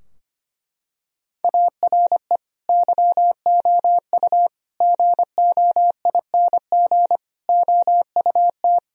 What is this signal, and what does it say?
Morse code, heard as a keyed tone, about 25 wpm, 700 Hz. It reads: ARE YOU GOING OUT
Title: Sentences from Top 1000 Words - Review Part 2 - 25wpm